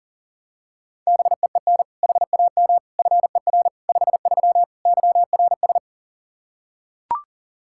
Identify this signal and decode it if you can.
Morse code, heard as a keyed tone, about 40 wpm, 700 Hz. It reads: BEEN HAM FER 53 YRS
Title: Practice QSO - Exchange 3 - More Info - 40wpm